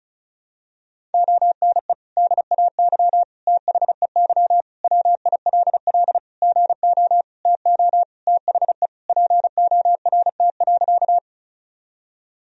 Morse code, heard as a keyed tone, about 36 wpm, 700 Hz. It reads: ONE DAY THEY WILL GO TO THE PORT.
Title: Sentences from Top 1000 Words - Review Part 1 - 35wpm